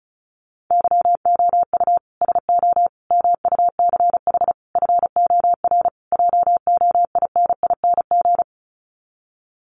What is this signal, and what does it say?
Morse code, heard as a keyed tone, about 34 wpm, 700 Hz. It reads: YOU SO MUCH FOR JOINING
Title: Sets of 5 Words Encore2 35wpm